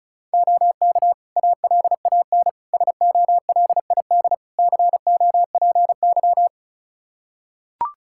Morse code, heard as a keyed tone, about 36 wpm, 700 Hz. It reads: OK ALAN SOLID COPY
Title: Practice QSO - Exchange 3 - More Info - 35wpm